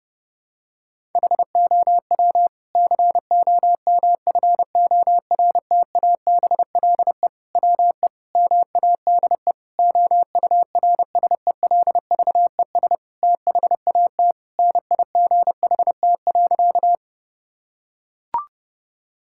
Morse code, heard as a keyed tone, about 30 wpm, 700 Hz. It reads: HOW COMFORTABLE WE MADE OURSELVES THAT NIGHT.
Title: Sentences from Top 1600 Words 30wpm